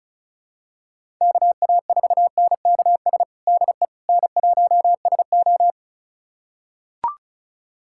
Morse code, heard as a keyed tone, about 34 wpm, 700 Hz. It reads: KA4NKS DE N1SO
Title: USA Callsign DE Callsign 35wpm